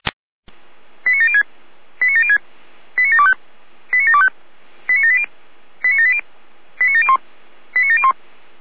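Five-tone selective call (ZVEI), heard as quick runs of five tones, tones of about 70 ms. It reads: 89706 89706 89725 89725 89780 89780 89701 89701
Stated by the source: ZVEI five-tone call received by SDR, from audio.f85360.feuerwehr-orlamuende.org (Feuerwehr Orlamünde fire service)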